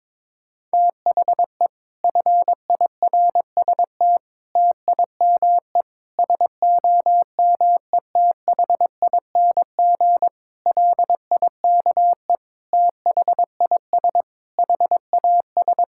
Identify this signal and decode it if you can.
Morse code, heard as a keyed tone, about 22 wpm, 700 Hz. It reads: THE FIRST TIME SOMETHING LIKE THIS HAS
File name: practice / 22wpm / Sets of 8 Words - 22wpm